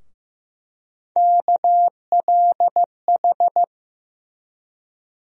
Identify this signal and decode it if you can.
Morse code, heard as a keyed tone, about 15 wpm, 700 Hz. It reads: KLH
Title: USA Call Sign Postfixes - 3 Letters 15wpm